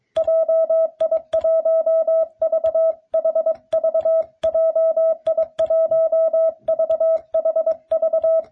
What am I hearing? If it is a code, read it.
Morse code, heard as a keyed tone, about 22 wpm, 660 Hz. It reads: JI1VHV JI1VHV